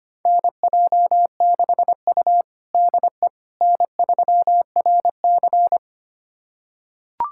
Morse code, heard as a keyed tone, about 25 wpm, 700 Hz. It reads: NJ6U DE N3RC
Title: USA Callsign DE Callsign - 4 Characters - 25wpm